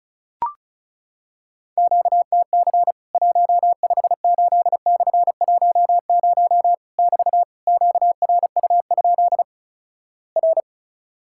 Morse code, heard as a keyed tone, about 36 wpm, 700 Hz. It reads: QTC 158/10 <BT> QRU? R
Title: Worked All Europe DX Contest - QTC Exchange - 35wpm